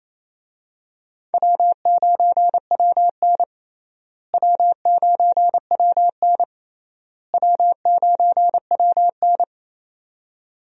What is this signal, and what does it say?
Morse code, heard as a keyed tone, about 28 wpm, 700 Hz. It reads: W9WN W9WN W9WN
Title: USA Call Signs - 4 Characters - Repeated 3x - 28wpm